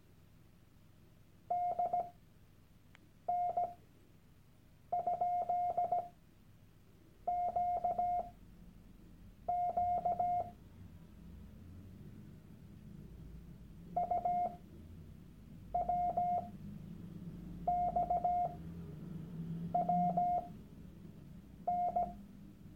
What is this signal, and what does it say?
Morse code, heard as a keyed tone, about 17 wpm, spaced as for 6 wpm, 710 Hz. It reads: DN?QQ UWXWN
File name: Workshop Dateien / CW-Online-Kurs / Lektion 001